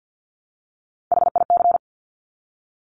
Morse code, heard as a keyed tone, about 50 wpm, 700 Hz. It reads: HIC
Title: Top Three Letter Combinations in Words 50wpm